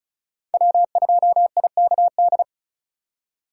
Morse code, heard as a keyed tone, about 36 wpm, 700 Hz. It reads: W2IKD
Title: USA Call Signs - 5 Characters - 6x Character Spacing - 35wpm